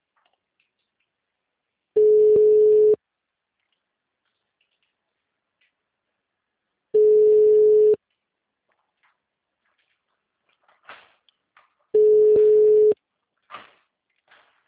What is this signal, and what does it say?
Ringback tone (Europe, 425 Hz).
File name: out- -2033- - - .28760